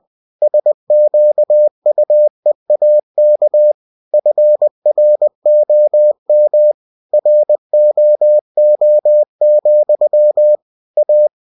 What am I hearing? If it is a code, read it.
Morse code, heard as a keyed tone, about 20 wpm, 600 Hz. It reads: SQUEAK FROM ROO, A
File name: cw-books / winnie-the-pooh / winnie-20wpm-page-0085